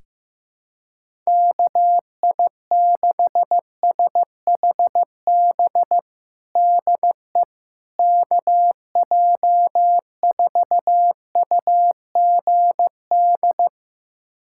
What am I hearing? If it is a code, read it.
Morse code, heard as a keyed tone, about 15 wpm, 700 Hz. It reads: KI6SHB DE KJ4UGD
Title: USA Callsign DE Callsign 15wpm